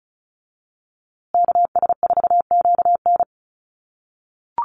Morse code, heard as a keyed tone, about 34 wpm, 700 Hz. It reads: KS4QN
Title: USA Call Signs Repeated 3x 35wpm